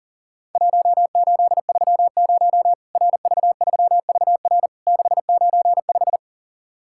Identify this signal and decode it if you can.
Morse code, heard as a keyed tone, about 40 wpm, 700 Hz. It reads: 1830 RV3VR 695
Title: Worked All Europe DX Contest - QTC Exchange - 40wpm